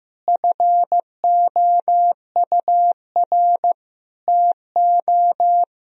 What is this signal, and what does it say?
Morse code, heard as a keyed tone, about 16 wpm, 700 Hz. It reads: FOUR TO
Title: Sentences from Top 1000 Words - Review Part 2 - 15wpm